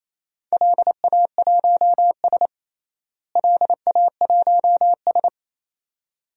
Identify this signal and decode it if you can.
Morse code, heard as a keyed tone, about 28 wpm, 700 Hz. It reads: LA1S LA1S
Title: DXCC-100 Call Signs - 4 Characters - Repeated 2x - 28wpm